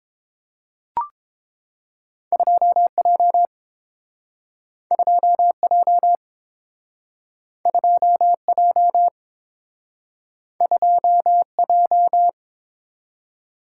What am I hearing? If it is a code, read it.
Morse code, heard as a keyed tone, about 30 wpm, 700 Hz. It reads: 2J 2J 2J 2J